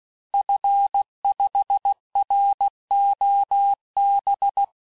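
Morse code, heard as a keyed tone, about 16 wpm, 800 Hz. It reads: F5ROB
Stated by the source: Morse course text, end of QSO, sent at 15 words per minute.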